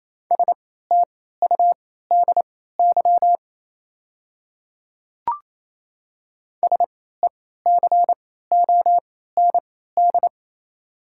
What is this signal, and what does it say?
Morse code, heard as a keyed tone, about 28 wpm, spaced as for 16 wpm, 690 Hz. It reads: STUDY SECOND
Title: 3x Character Spacing - Top 500 Words - 28wpm